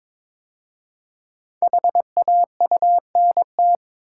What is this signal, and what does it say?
Morse code, heard as a keyed tone, about 22 wpm, 700 Hz. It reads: HAUNT
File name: 5 Letter Words - First 3 Letters Given - 22wpm